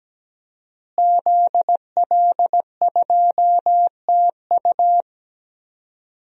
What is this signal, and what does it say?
Morse code, heard as a keyed tone, about 17 wpm, 700 Hz. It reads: ZL2TU
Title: DXCC-100 Call Signs - 5 Characters - Repeated 3x - 17wpm